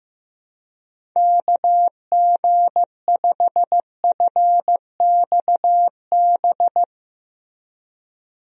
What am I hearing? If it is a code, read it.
Morse code, heard as a keyed tone, about 16 wpm, 700 Hz. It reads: KG5FXB
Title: USA Call Signs - 6 Characters - Repeated 3x - 15wpm